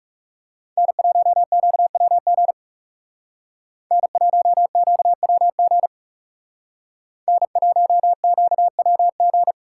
Morse code, heard as a keyed tone, about 38 wpm, 700 Hz. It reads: N1QWG N1QWG N1QWG